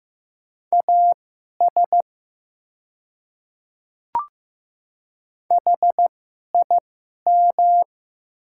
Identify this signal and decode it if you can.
Morse code, heard as a keyed tone, about 15 wpm, spaced as for 11 wpm, 700 Hz. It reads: AS HIM
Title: 2x Character Spacing - Top 100 Words - 15wpm